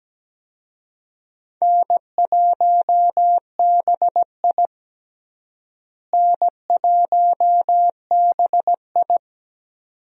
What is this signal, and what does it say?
Morse code, heard as a keyed tone, about 17 wpm, 700 Hz. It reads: N1BI N1BI
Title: USA Call Signs - 4 Characters - Repeated 2x - 17wpm